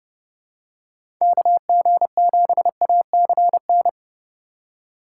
Morse code, heard as a keyed tone, about 30 wpm, 700 Hz. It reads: KG7ACN
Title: USA Call Signs - 6 Characters - 4x Character Spacing - 30wpm